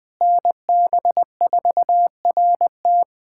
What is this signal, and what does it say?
Morse code, heard as a keyed tone, about 20 wpm, 700 Hz. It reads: NB4RT
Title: USA Callsign DE Callsign 20wpm